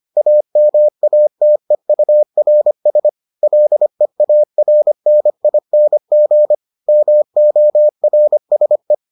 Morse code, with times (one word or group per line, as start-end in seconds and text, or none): 0.2-3.1 AMATEURS
3.4-6.5 LEARNING
6.9-8.9 MORSE